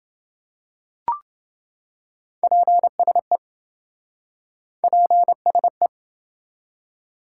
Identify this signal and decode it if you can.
Morse code, heard as a keyed tone, about 27 wpm, 700 Hz. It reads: PSE PSE